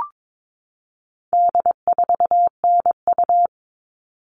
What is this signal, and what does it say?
Morse code, heard as a keyed tone, about 22 wpm, 700 Hz. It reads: D4NU
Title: All Cut Numbers - 4-Digit 22wpm